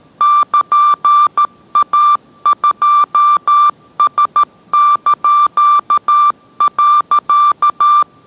Morse code, heard as a keyed tone, about 13 wpm, 1180 Hz. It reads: <KN>A2S).